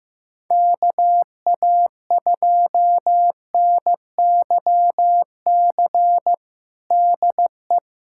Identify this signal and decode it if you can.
Morse code, heard as a keyed tone, about 15 wpm, 700 Hz. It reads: KA2NYC DE
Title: USA Callsign DE Callsign - 6 Characters - 15wpm